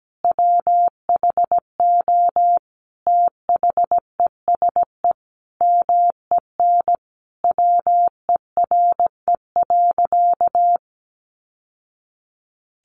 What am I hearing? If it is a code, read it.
Morse code, heard as a keyed tone, about 17 wpm, 700 Hz. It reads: WHO THESE MEN WERE.
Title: Sentences from Top 1000 Words 17wpm